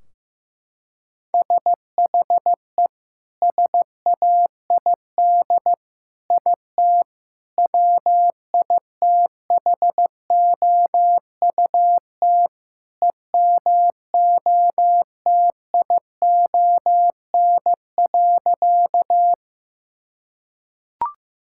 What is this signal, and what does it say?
Morse code, heard as a keyed tone, about 15 wpm, 700 Hz. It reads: SHE SAID IT WITHOUT EMOTION.
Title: Sentences from Top 1600 Words 15wpm